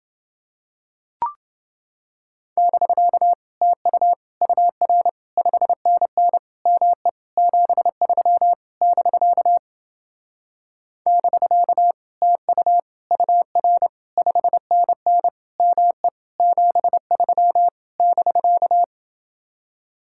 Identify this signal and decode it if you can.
Morse code, heard as a keyed tone, about 28 wpm, 700 Hz. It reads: <BK> TU UR 5NN ME 73 <BK> <BK> TU UR 5NN ME 73 <BK>